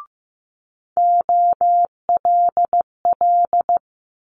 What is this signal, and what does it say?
Morse code, heard as a keyed tone, about 15 wpm, 700 Hz. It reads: OLL
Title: Multi-letter Phonemes Quick Fire 15wpm